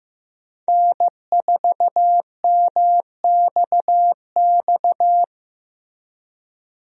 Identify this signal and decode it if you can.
Morse code, heard as a keyed tone, about 15 wpm, 700 Hz. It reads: N4MXX
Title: USA Call Signs - 5 Characters - Repeated 3x - 15wpm